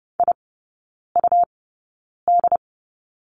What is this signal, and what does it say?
Morse code, heard as a keyed tone, about 30 wpm, spaced as for 9 wpm, 700 Hz. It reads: IUD
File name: Lesson 19e - Letter w - Three Characters 30wpm